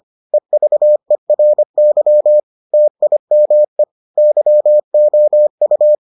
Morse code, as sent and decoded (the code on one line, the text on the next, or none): . ...- . .-. -.-- / - .. -- . / -.-- --- ..-
EVERY TIME YOU